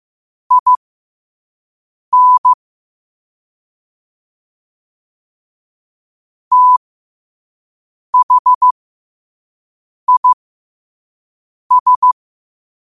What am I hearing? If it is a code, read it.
Morse code, heard as a keyed tone, about 15 wpm, spaced as for 5 wpm, 1000 Hz. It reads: IN THIS
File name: qotdfiles / 5wpm